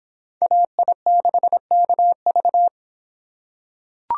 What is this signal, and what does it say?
Morse code, heard as a keyed tone, about 26 wpm, 700 Hz. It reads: AI6KV